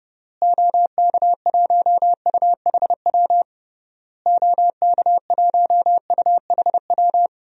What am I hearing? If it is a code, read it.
Morse code, heard as a keyed tone, about 30 wpm, 700 Hz. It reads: OK1UHW OK1UHW
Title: DXCC-100 Call Signs - 6 Characters - Repeated 3x - 30wpm